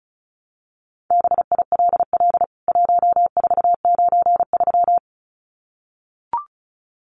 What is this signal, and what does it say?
Morse code, heard as a keyed tone, about 34 wpm, 700 Hz. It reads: BILL 1493
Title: CWT Exchange 35wpm